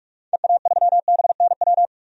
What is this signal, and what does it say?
Morse code, heard as a keyed tone, about 44 wpm, 690 Hz. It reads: EA3BNW
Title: DXCC-100 Call Signs - 6 Characters - 45wpm